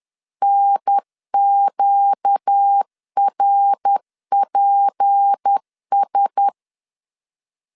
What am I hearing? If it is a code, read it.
Morse code, heard as a keyed tone, about 10 wpm, 800 Hz. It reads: NQRPS
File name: LEZIONI / 6 PKX0 / 6 PK 10wpm 2436